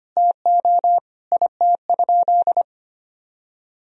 Morse code, heard as a keyed tone, about 25 wpm, 700 Hz. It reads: TO IT?
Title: Sentences from Top 1000 Words - Review Part 3 - 25wpm